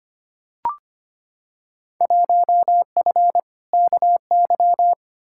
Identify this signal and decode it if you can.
Morse code, heard as a keyed tone, about 25 wpm, 700 Hz. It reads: 1F KY
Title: Field Day Exchanges Encore - 25wpm